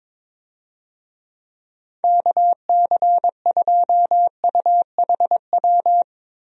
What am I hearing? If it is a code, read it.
Morse code, heard as a keyed tone, about 22 wpm, 700 Hz. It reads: KC2UHW
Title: USA Call Signs - 6 Characters - 22wpm